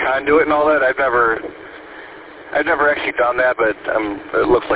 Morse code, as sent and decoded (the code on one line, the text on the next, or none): none